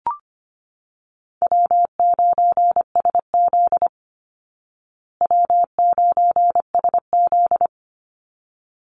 Morse code, heard as a keyed tone, about 25 wpm, 700 Hz. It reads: W9SZ W9SZ
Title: Call Signs Repeated 3x - 25wpm